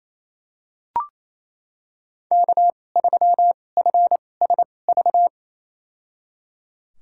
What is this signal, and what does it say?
Morse code, heard as a keyed tone, about 28 wpm, spaced as for 20 wpm, 700 Hz. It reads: K3FSV
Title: USA Call Signs - 5 Characters - 2x Character Spacing - 28wpm